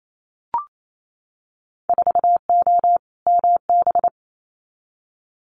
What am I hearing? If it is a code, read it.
Morse code, heard as a keyed tone, about 28 wpm, 700 Hz. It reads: 4O MB
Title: Winter Field Day Exchanges Encore 28wpm